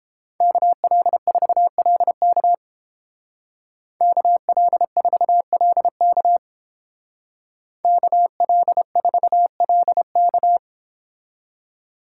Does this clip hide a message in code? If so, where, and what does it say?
Yes: Morse code from 0.4 to 10.6 s, heard as a keyed tone, about 31 wpm, 700 Hz. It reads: KL4LK KL4LK KL4LK